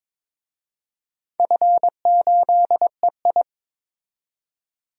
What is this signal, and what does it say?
Morse code, heard as a keyed tone, about 22 wpm, 700 Hz. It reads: F8EI
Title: DXCC-100 Call Signs - 4 Characters - 22wpm